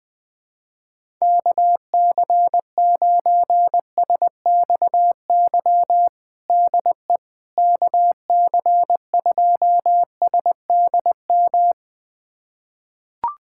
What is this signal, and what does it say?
Morse code, heard as a keyed tone, about 20 wpm, 700 Hz. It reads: KC9SXY DE KC2SDM
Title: USA Callsign DE Callsign - 6 Characters - 20wpm